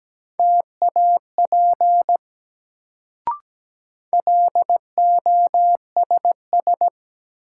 Morse code, heard as a keyed tone, about 17 wpm, 700 Hz. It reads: TAP LOSS
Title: Sentences from Top 1400 Words - New Word Drill - 17wpm